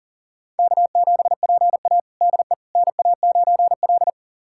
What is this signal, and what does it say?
Morse code, heard as a keyed tone, about 40 wpm, 700 Hz. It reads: K7PA DE NA9L
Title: USA Callsign DE Callsign - 4 Characters - 40wpm